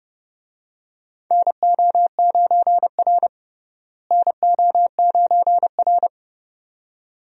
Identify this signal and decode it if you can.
Morse code, heard as a keyed tone, about 31 wpm, 700 Hz. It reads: NO9R NO9R
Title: USA Call Signs - 4 Characters - Repeated 2x - 30wpm